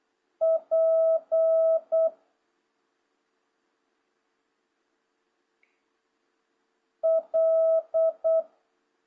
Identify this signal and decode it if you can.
Morse code, heard as a keyed tone, about 8 wpm, 650 Hz. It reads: P L
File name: jh8cbh / 2026cw